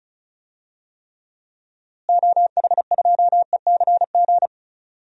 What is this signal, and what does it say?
Morse code, heard as a keyed tone, about 36 wpm, 700 Hz. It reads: OH2ECG
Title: DXCC-100 Call Signs - 6 Characters - Repeated 3x - 35wpm